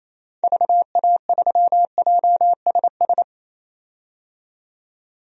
Morse code, heard as a keyed tone, about 28 wpm, 700 Hz. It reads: VA3JSS
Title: DXCC-100 Call Signs - 6 Characters - 28wpm